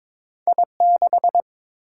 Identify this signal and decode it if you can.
Morse code, heard as a keyed tone, about 22 wpm, 700 Hz. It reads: I6
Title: Single-Letter Single-Number 22wpm